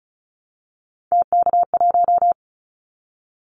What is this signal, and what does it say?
Morse code, heard as a keyed tone, about 34 wpm, 700 Hz. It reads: TK1
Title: Callsign Prefixes for Top 100 DXCC - 35wpm